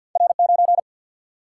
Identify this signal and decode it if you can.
Morse code, heard as a keyed tone, about 50 wpm, 700 Hz. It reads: R 9
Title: Single-Letter Single-Number 50wpm